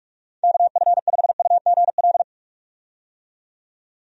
Morse code, heard as a keyed tone, about 43 wpm, 700 Hz. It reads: KF5UGL
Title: USA Call Signs - 6 Characters - 45wpm